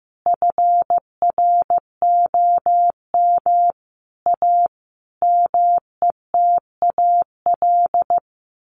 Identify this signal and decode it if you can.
Morse code, heard as a keyed tone, about 15 wpm, 700 Hz. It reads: FROM A METAL
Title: A Sound Of Thunder 15wpm